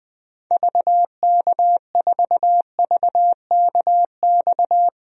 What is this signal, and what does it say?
Morse code, heard as a keyed tone, about 20 wpm, 700 Hz. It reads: VK4VKX
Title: DXCC-100 Call Signs - 6 Characters - Repeated 2x - 20wpm